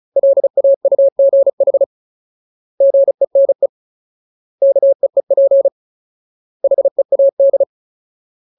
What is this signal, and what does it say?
Morse code, heard as a keyed tone, about 34 wpm, 550 Hz. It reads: LAUGH GENE KEEP HEAD
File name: Ultimate Word List / 35WPM Ultimate